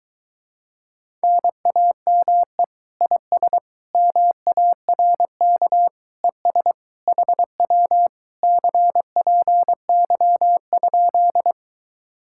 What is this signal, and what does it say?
Morse code, heard as a keyed tone, about 23 wpm, 700 Hz. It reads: NAME IS MARK ES HW CPY?